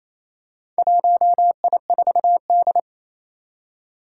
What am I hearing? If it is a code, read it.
Morse code, heard as a keyed tone, about 27 wpm, 700 Hz. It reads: 1I4D
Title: Four Challenging Letter-Number Combinations 28wpm